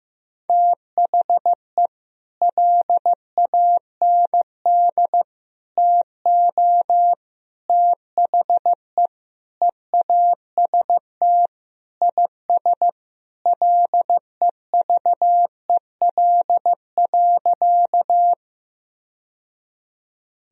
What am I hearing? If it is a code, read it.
Morse code, heard as a keyed tone, about 15 wpm, 700 Hz. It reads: THE LAND TO THE EAST IS LEVEL.
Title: Sentences from Top 1000 Words - Review Part 4 - 15wpm